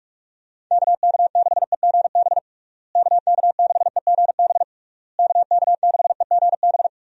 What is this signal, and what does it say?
Morse code, heard as a keyed tone, about 41 wpm, 700 Hz. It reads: KK6EGB KK6EGB KK6EGB
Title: USA Call Signs - 6 Characters - Repeated 3x - 45wpm